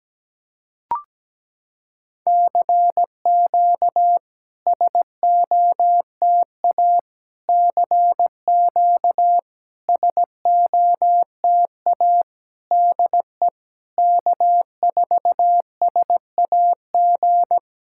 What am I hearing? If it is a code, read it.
Morse code, heard as a keyed tone, about 17 wpm, 700 Hz. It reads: CQ SOTA CQ SOTA DE K4SAG
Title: SOTA QSO Practice 17wpm